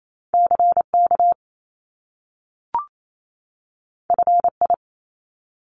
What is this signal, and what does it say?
Morse code, heard as a keyed tone, about 28 wpm, 700 Hz. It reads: CK FI
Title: Top Two Letter Combinations in Words 28wpm